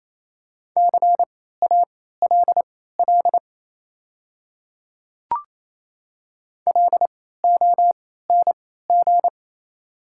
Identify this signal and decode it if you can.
Morse code, heard as a keyed tone, about 27 wpm, spaced as for 16 wpm, 690 Hz. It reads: CALL LONG
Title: 3x Character Spacing - Top 100 Words - 28wpm